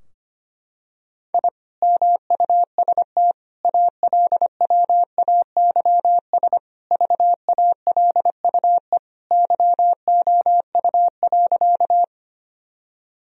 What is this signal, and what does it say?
Morse code, heard as a keyed tone, about 25 wpm, 700 Hz. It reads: I MUST ALWAYS VALUE YOU.
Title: Sentences from Top 1000 Words - Review Part 3 - 25wpm